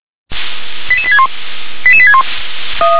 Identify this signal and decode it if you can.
Five-tone selective call (ZVEI), heard as quick runs of five tones, tones of about 70 ms. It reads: 88861 88861
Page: /357/SDR/ZVEI/